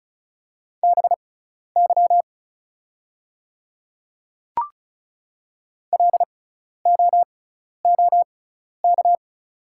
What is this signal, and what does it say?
Morse code, heard as a keyed tone, about 36 wpm, spaced as for 12 wpm, 700 Hz. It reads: BY LOOK
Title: 6x Character Spacing - Top 100 Words - 35wpm